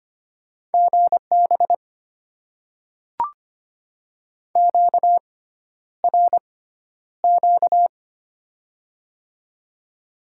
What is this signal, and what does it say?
Morse code, heard as a keyed tone, about 25 wpm, spaced as for 11 wpm, 700 Hz. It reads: GB QRQ
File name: Top 90 QSO Elements - 6x Character Spacing - 25wpm